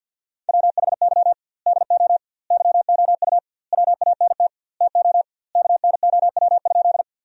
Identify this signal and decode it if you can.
Morse code, heard as a keyed tone, about 50 wpm, 700 Hz. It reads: WHY DO YOU WANT TO KNOW?
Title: 2x Word-Spacing - Sentences from Top 200 Words - 50wpm